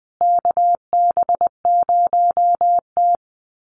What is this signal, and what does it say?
Morse code, heard as a keyed tone, about 20 wpm, 700 Hz. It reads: KB0T
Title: USA Call Signs Repeated 3x 20wpm